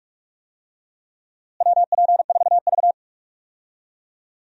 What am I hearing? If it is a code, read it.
Morse code, heard as a keyed tone, about 46 wpm, 700 Hz. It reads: WP4V
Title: USA Call Signs - 4 Characters - Repeated 3x - 45wpm